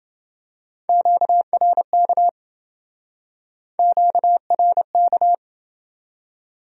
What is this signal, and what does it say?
Morse code, heard as a keyed tone, about 29 wpm, 700 Hz. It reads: QRK QRK